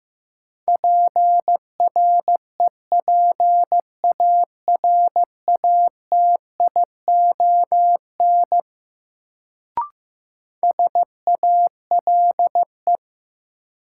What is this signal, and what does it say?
Morse code, heard as a keyed tone, about 15 wpm, 700 Hz. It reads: PREPARATION SALE
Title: Sentences from Top 1400 Words - New Word Drill - 15wpm